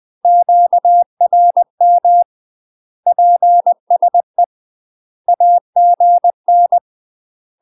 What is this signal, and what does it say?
Morse code, heard as a keyed tone, about 20 wpm, 700 Hz. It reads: QRM PSE AGN
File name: UK MASTER QSOs TRAINING / 20WPM / UK MASTER QSOs 0062